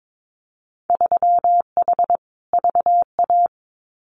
Morse code, heard as a keyed tone, about 22 wpm, 700 Hz. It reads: 3H VA
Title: Winter Field Day Exchanges 22wpm